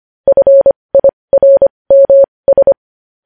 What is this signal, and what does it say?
Morse code, heard as a keyed tone, about 25 wpm, spaced as for 20 wpm, 550 Hz. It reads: FIRMS
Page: news headlines in morse code at 20 wpm